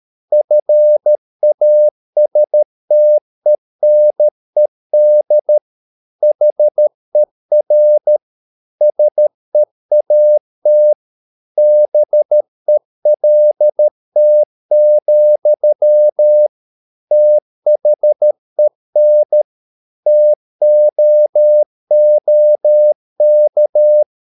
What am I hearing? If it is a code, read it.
Morse code, heard as a keyed tone, about 13 wpm, 600 Hz. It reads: FASTENED HER SEAT BELT, THEN TOOK